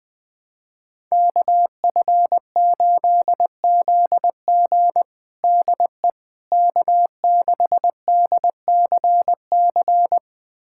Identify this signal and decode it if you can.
Morse code, heard as a keyed tone, about 20 wpm, 700 Hz. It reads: KF8ZG DE K6DCC
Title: USA Callsign DE Callsign - 5 Characters - 20wpm